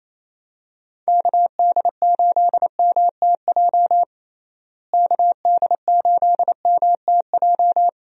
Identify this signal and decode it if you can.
Morse code, heard as a keyed tone, about 28 wpm, 700 Hz. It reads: KD8MTJ KD8MTJ
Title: USA Call Signs - 6 Characters - Repeated 3x - 28wpm